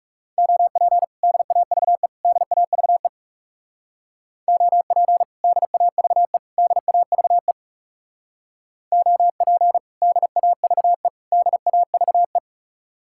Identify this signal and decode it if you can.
Morse code, heard as a keyed tone, about 36 wpm, 700 Hz. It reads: OP DAVE DAVE OP DAVE DAVE OP DAVE DAVE
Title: Speed-Racing Common QSO Phrases - 30wpm